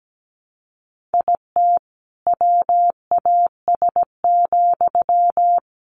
Morse code, heard as a keyed tone, about 17 wpm, 700 Hz. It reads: IT WAS,
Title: Sentences from Top 600 Words 17wpm